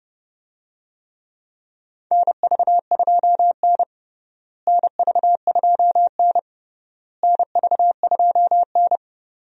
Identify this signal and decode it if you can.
Morse code, heard as a keyed tone, about 30 wpm, 700 Hz. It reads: NV2N NV2N NV2N
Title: USA Call Signs - 4 Characters - Repeated 3x - 30wpm